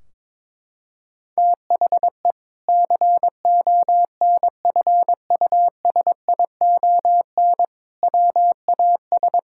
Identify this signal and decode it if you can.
Morse code, heard as a keyed tone, about 22 wpm, 700 Hz. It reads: THE CONFUSION WAS
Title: Sentences from Top 1600 Words 22wpm